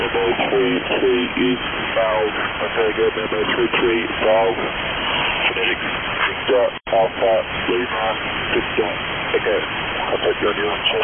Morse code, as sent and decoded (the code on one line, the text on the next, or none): none